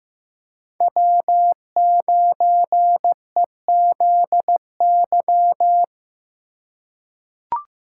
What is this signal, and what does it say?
Morse code, heard as a keyed tone, about 15 wpm, 700 Hz. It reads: W9EZY
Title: USA Call Signs - 5 Characters - 2x Character Spacing - 15wpm